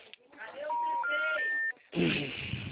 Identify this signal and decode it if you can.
Special information tone, with speech.